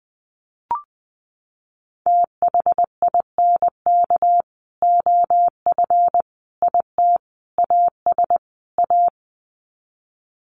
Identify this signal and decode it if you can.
Morse code, heard as a keyed tone, about 20 wpm, 700 Hz. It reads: THINK OF IT AS A
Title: Sets of 5 Words Encore2 20wpm